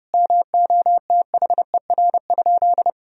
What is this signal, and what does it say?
Morse code, heard as a keyed tone, about 30 wpm, 700 Hz. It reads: MOTHER?
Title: Sentences from Top 200 Words and Abbreviations 30wpm